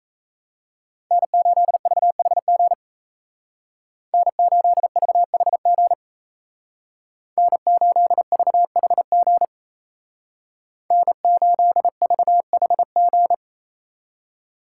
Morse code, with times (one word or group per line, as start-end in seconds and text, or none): 1.1-2.7 N8VHG
4.1-5.9 N8VHG
7.4-9.5 N8VHG
10.9-13.3 N8VHG